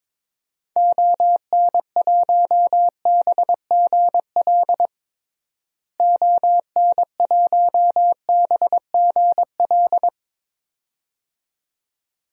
Morse code, heard as a keyed tone, about 22 wpm, 700 Hz. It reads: ON1BGL ON1BGL
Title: DXCC-100 Call Signs - 6 Characters - Repeated 2x - 22wpm